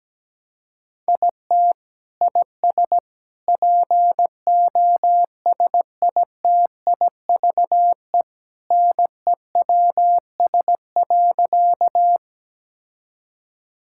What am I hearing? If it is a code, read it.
Morse code, heard as a keyed tone, about 17 wpm, 700 Hz. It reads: IT IS POSITIVE NEWS.
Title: Sentences from Top 1000 Words - Review Part 5 - 17wpm